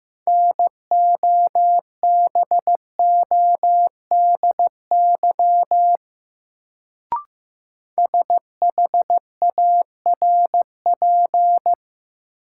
Morse code, heard as a keyed tone, about 15 wpm, 700 Hz. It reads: NOBODY SHARP
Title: Sentences from Top 1600 Words - New Word Drill - 15wpm